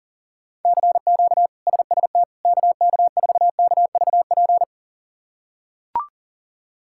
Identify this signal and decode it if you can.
Morse code, heard as a keyed tone, about 40 wpm, 700 Hz. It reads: CQ SST KK4KVP
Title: K1USN SST QSO Practice - 40wpm